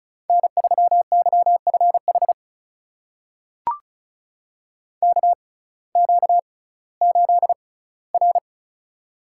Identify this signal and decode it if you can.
Morse code, heard as a keyed tone, about 36 wpm, spaced as for 19 wpm, 700 Hz. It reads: N3YFH KQ8R
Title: USA Call Signs - 6x Character Spacing - 35wpm